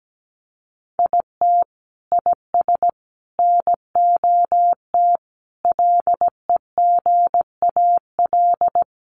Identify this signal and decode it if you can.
Morse code, heard as a keyed tone, about 17 wpm, 700 Hz. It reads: IT IS NOT LEGAL
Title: Sentences from Top 1200 Words 17wpm